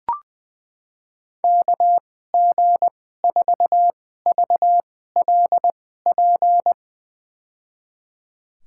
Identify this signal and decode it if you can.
Morse code, heard as a keyed tone, about 20 wpm, spaced as for 15 wpm, 700 Hz. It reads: KG4VLP
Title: USA Call Signs - 6 Characters - 2x Character Spacing - 20wpm